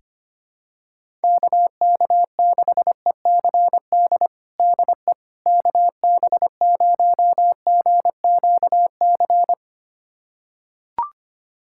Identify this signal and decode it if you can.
Morse code, heard as a keyed tone, about 25 wpm, 700 Hz. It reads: KK6ECD DE KB0GQC
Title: USA Callsign DE Callsign - 6 Characters - 25wpm